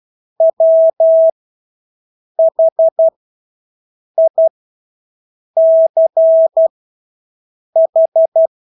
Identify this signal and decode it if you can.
Morse code, heard as a keyed tone, about 12 wpm, spaced as for 6 wpm, 650 Hz. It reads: WHICH
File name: Engelske ord 12-6 wpm